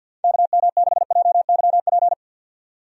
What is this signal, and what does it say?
Morse code, heard as a keyed tone, about 50 wpm, 700 Hz. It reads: KM6JYP
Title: USA Call Signs - 6 Characters - 50wpm